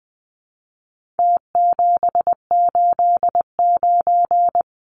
Morse code, heard as a keyed tone, about 20 wpm, 700 Hz. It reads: T789
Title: Numbers - Contest Style - 4-Digit 20wpm